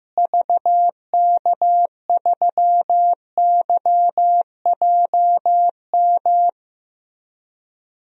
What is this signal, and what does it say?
Morse code, heard as a keyed tone, about 15 wpm, 700 Hz. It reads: VK3YJM
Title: DXCC-100 Call Signs - 6 Characters - Repeated 2x - 15wpm